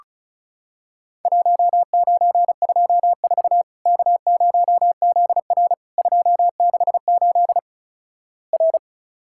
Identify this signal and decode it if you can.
Morse code, heard as a keyed tone, about 36 wpm, 700 Hz. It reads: 1924 K0ZR 268 R
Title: Worked All Europe DX Contest - QTC Exchange - 35wpm